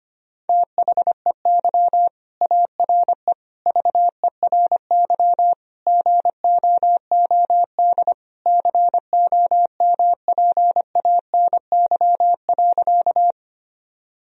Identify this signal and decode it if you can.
Morse code, heard as a keyed tone, about 25 wpm, 700 Hz. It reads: THEY ARE VERY GOOD COMPANY.
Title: Sentences from Top 1000 Words - Review Part 3 - 25wpm